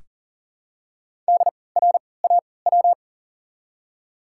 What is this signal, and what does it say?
Morse code, heard as a keyed tone, about 40 wpm, spaced as for 23 wpm, 700 Hz. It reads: DRAW
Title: 3x Character Spacing - Top 300 Words - 40wpm